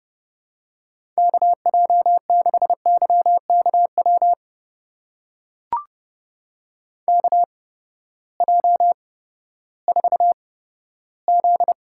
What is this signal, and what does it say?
Morse code, heard as a keyed tone, about 30 wpm, spaced as for 15 wpm, 700 Hz. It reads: KJ6YKW KJ4Z
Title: USA Call Signs - 6 Characters - 8x Character Spacing - 30wpm